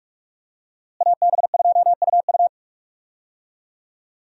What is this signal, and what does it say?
Morse code, heard as a keyed tone, about 46 wpm, 700 Hz. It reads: AB2UU
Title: USA Call Signs - 5 Characters - Repeated 3x - 45wpm